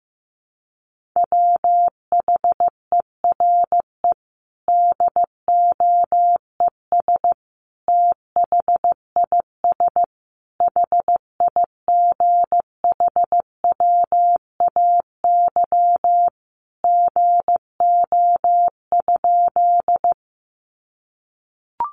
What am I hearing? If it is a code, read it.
Morse code, heard as a keyed tone, about 15 wpm, 700 Hz. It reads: WHERE DOES THIS HIGHWAY GO?
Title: Sentences from Top 1400 Words 15wpm